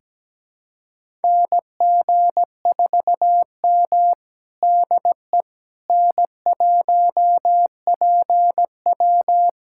Morse code, heard as a keyed tone, about 17 wpm, 700 Hz. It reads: NG4M DE N1PW
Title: USA Callsign DE Callsign - 4 Characters - 17wpm